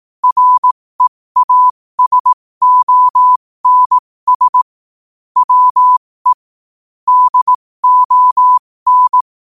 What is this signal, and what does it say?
Morse code, heard as a keyed tone, about 18 wpm, spaced as for 15 wpm, 1000 Hz. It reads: REASONS WE DON
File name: qotdfiles / 15wpm